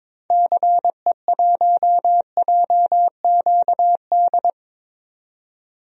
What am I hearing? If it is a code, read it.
Morse code, heard as a keyed tone, about 22 wpm, 700 Hz. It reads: CE1JQD
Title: DXCC-100 Call Signs - 6 Characters - 22wpm